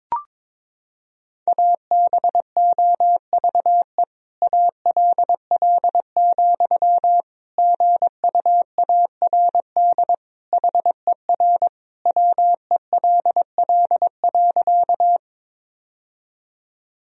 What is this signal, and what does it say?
Morse code, heard as a keyed tone, about 22 wpm, 700 Hz. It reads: ABOVE ALL, GUARD HER WELL.
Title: Sentences from Top 1000 Words - Review Part 4 - 22wpm